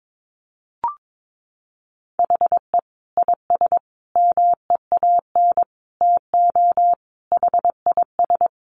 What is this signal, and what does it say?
Morse code, heard as a keyed tone, about 22 wpm, 700 Hz. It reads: HE IS MEAN TO HIS
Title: Additional Easy Phrases 22wpm